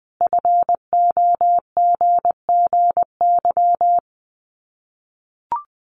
Morse code, heard as a keyed tone, about 20 wpm, 700 Hz. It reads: FOGGY
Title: Common QSO Phrases 20wpm